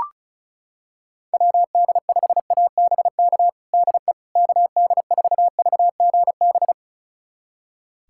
Morse code, heard as a keyed tone, about 36 wpm, 700 Hz. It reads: WD5ABK DE KD4VGB
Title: USA Callsign DE Callsign 35wpm